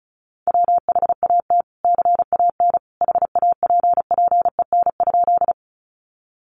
Morse code, heard as a keyed tone, about 34 wpm, 700 Hz. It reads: WHAT CAN HAPPEN?
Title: Sentences from Top 400 Words 35wpm